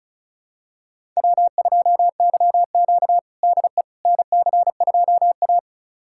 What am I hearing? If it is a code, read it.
Morse code, heard as a keyed tone, about 36 wpm, 700 Hz. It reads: W2YQ DE NC2A
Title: USA Callsign DE Callsign - 4 Characters - 35wpm